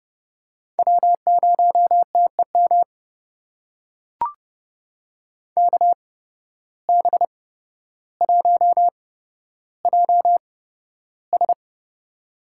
Morse code, heard as a keyed tone, about 30 wpm, spaced as for 13 wpm, 700 Hz. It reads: W0TEM KB1JS
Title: USA Call Signs - 5 Characters - 8x Character Spacing - 30wpm